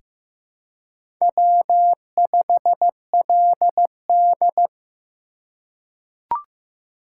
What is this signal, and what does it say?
Morse code, heard as a keyed tone, about 15 wpm, 690 Hz. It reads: W5LD
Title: USA Call Signs - 4 Characters - 2x Character Spacing - 15wpm